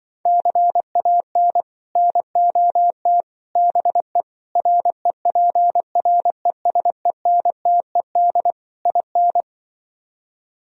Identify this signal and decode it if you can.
Morse code, heard as a keyed tone, about 24 wpm, 700 Hz. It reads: CAN NOT BE REPRESENTED IN
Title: Speed-Racing - Sets of 5 Words 20wpm